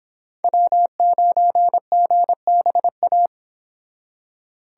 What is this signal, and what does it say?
Morse code, heard as a keyed tone, about 25 wpm, 700 Hz. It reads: W9GBA